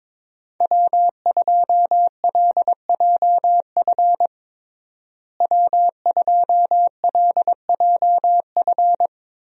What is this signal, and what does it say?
Morse code, heard as a keyed tone, about 22 wpm, 700 Hz. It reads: W2LJF W2LJF
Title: USA Call Signs - 5 Characters - Repeated 2x - 22wpm